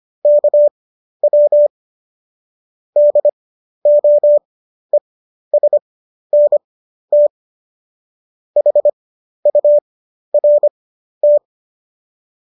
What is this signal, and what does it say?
Morse code, heard as a keyed tone, about 25 wpm, spaced as for 12 wpm, 590 Hz. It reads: KW DOESNT HURT